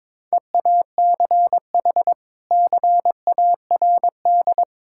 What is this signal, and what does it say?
Morse code, heard as a keyed tone, about 22 wpm, 700 Hz. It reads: EACH CARD
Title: Sentences from Top 1000 Words - Review Part 6 - 22wpm